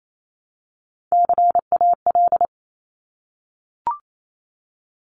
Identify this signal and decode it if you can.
Morse code, heard as a keyed tone, about 28 wpm, 700 Hz. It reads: CAL
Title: Top Three Letter Combinations in Words 28wpm